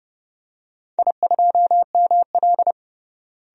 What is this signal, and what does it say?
Morse code, heard as a keyed tone, about 32 wpm, 700 Hz. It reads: I2ML
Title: DXCC-100 Call Signs - 4 Characters - Repeated 2x - 30wpm